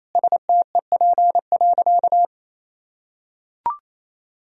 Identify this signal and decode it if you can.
Morse code, heard as a keyed tone, about 28 wpm, 700 Hz. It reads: STEP.
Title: Sentences from Top 1000 Words - Review Part 2 - 28wpm